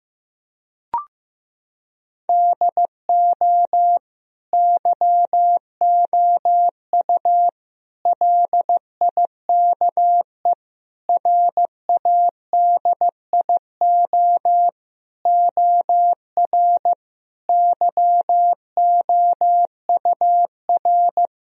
Morse code, heard as a keyed tone, about 15 wpm, 700 Hz. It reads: DO YOU LIKE RADIO OR YOUR
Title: Sentences from Top 1000 Words - Review Part 4 - 15wpm